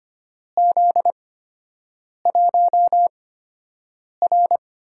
Morse code, heard as a keyed tone, about 25 wpm, spaced as for 7 wpm, 700 Hz. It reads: Z1R
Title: USA Call Signs - 8x Character Spacing - 25wpm